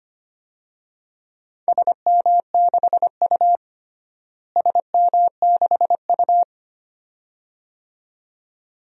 Morse code, heard as a keyed tone, about 25 wpm, 700 Hz. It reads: SM6U SM6U
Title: DXCC-100 Call Signs - 4 Characters - Repeated 2x - 25wpm